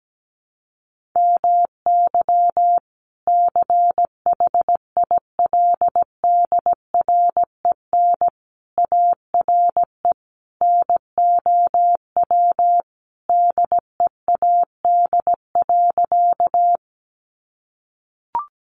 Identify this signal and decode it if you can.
Morse code, heard as a keyed tone, about 17 wpm, 700 Hz. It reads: MY CHILDREN ARE NOW DEAD.
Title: Sentences from Top 1500 Words 17wpm